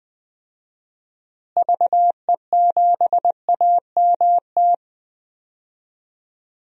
Morse code, heard as a keyed tone, about 20 wpm, 700 Hz. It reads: VE7AMT
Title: DXCC-100 Call Signs - 6 Characters - Repeated 2x - 20wpm